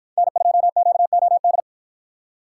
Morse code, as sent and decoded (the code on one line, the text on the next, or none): -. ..--- --.- --- -..
N2QOD